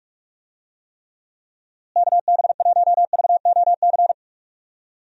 Morse code, heard as a keyed tone, about 45 wpm, 700 Hz. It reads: KB1VOC
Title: USA Call Signs - 6 Characters - Repeated 2x - 45wpm